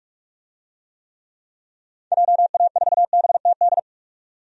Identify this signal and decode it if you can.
Morse code, heard as a keyed tone, about 42 wpm, 700 Hz. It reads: JA4BTD
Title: DXCC-100 Call Signs - 6 Characters - 45wpm